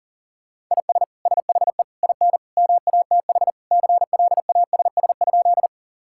Morse code, heard as a keyed tone, about 37 wpm, 700 Hz. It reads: IS SHE IN MATH CLASS?
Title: Sentences from Top 1000 Words - Review Part 6 - 40wpm